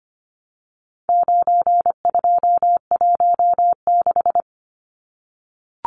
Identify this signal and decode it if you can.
Morse code, heard as a keyed tone, about 25 wpm, 700 Hz. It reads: 9216
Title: Numbers - 4-Digit 25wpm